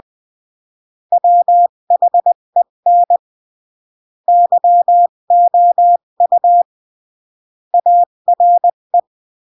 Morse code, heard as a keyed tone, about 20 wpm, 700 Hz. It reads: WHEN YOU ARE
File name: CW / jodorowsky / 82 commandments 20-15